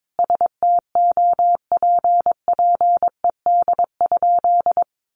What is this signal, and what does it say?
Morse code, heard as a keyed tone, about 22 wpm, 700 Hz. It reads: STOPPED?
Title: A Sound Of Thunder 22wpm